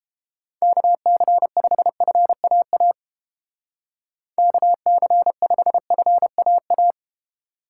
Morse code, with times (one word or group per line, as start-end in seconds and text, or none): 0.6-2.9 KC5FAA
4.4-6.9 KC5FAA